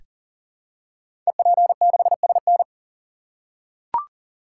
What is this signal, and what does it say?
Morse code, heard as a keyed tone, about 39 wpm, 700 Hz. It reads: EP6SN
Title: Code Groups - 5 Letter:Numbers - 40wpm